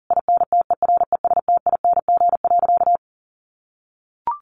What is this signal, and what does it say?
Morse code, heard as a keyed tone, about 40 wpm, 700 Hz. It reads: INTERESTING.
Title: Sentences from Top 1000 Words 40wpm